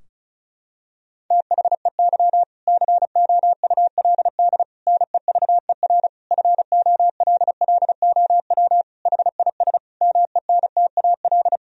Morse code, heard as a keyed tone, about 36 wpm, 700 Hz. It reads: THEY COULD NEVER FOLLOW HIS MENTAL
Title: Sentences from Top 1000 Words - Review Part 4 - 35wpm